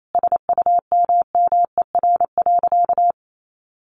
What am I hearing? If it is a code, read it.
Morse code, heard as a keyed tone, about 28 wpm, 700 Hz. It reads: SUMMER.
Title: Sentences from Top 1100 Words 28wpm